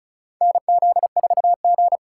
Morse code, heard as a keyed tone, about 36 wpm, 700 Hz. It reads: NZ4G
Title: USA Call Signs - 4 Characters - 4x Character Spacing - 35wpm